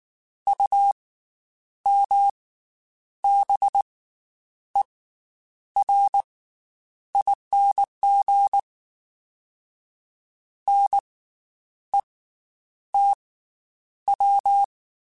Morse code, heard as a keyed tone, about 19 wpm, 800 Hz. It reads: U M B E R ING N E T W
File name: Special INGs / Special INGs-mp3s / Slow ing @18